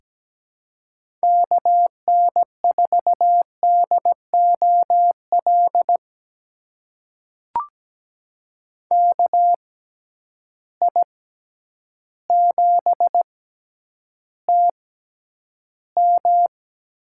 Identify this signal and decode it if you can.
Morse code, heard as a keyed tone, about 17 wpm, spaced as for 9 wpm, 700 Hz. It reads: KN4DOL KI7TM
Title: USA Call Signs - 6 Characters - 6x Character Spacing - 17wpm